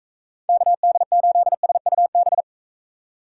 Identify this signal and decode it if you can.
Morse code, heard as a keyed tone, about 38 wpm, 700 Hz. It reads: KD8SUB